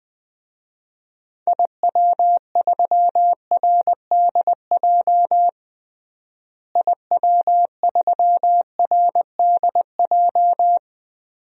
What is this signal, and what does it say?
Morse code, heard as a keyed tone, about 20 wpm, 700 Hz. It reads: IW3RDJ IW3RDJ
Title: DXCC-100 Call Signs - 6 Characters - Repeated 2x - 20wpm